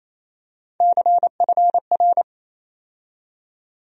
Morse code, heard as a keyed tone, about 27 wpm, 700 Hz. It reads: CFR
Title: USA Call Sign Postfixes - 3 Letters 28wpm